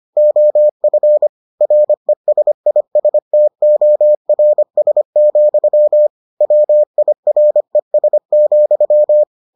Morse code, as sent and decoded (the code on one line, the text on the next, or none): --- ..-. / .-. ... .. ... - --- .-. ... --..-- / .-- .. .-. ... --..--
OF RESISTORS, WIRES,